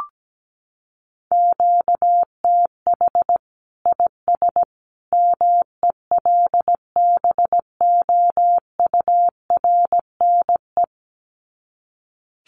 Morse code, heard as a keyed tone, about 17 wpm, 700 Hz. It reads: QTH IS MELBOURNE
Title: Common QSO Phrases 17wpm